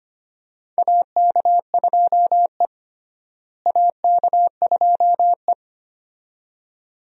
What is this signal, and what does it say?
Morse code, heard as a keyed tone, about 25 wpm, 700 Hz. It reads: AK2E AK2E
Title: USA Call Signs - 4 Characters - Repeated 2x - 25wpm